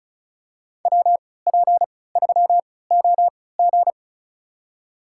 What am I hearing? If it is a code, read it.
Morse code, heard as a keyed tone, about 34 wpm, spaced as for 20 wpm, 700 Hz. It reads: WP3OG
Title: USA Call Signs - 5 Characters - 3x Character Spacing - 35wpm